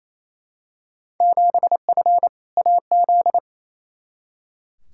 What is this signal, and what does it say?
Morse code, heard as a keyed tone, about 28 wpm, 700 Hz. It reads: 7F AZ
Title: Field Day Exchanges Encore - 28wpm